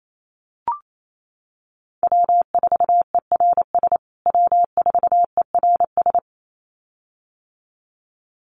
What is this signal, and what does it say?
Morse code, heard as a keyed tone, about 28 wpm, 700 Hz. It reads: W4ERS W4ERS
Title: USA Call Signs Repeated 2x 28wpm